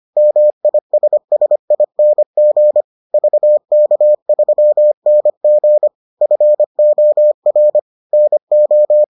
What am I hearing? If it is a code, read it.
Morse code, heard as a keyed tone, about 25 wpm, 600 Hz. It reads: MISSING VK3NG FOR NO